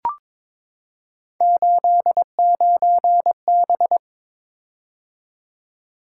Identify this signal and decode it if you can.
Morse code, heard as a keyed tone, about 22 wpm, 700 Hz. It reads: 89B
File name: practice / 22wpm / Three Letter-Number Combinations - 22wpm